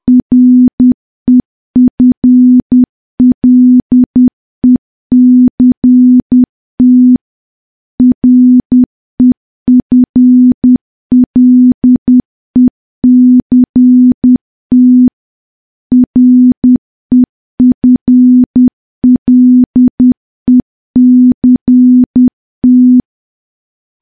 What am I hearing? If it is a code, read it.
Morse code, heard as a keyed tone, about 10 wpm, 250 Hz. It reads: REFLECT REFLECT REFLECT
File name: morse / morse-training-10WPM